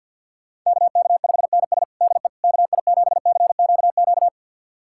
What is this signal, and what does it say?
Morse code, heard as a keyed tone, about 46 wpm, 700 Hz. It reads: KK5NS DE KI7CQQ
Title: USA Callsign DE Callsign 50wpm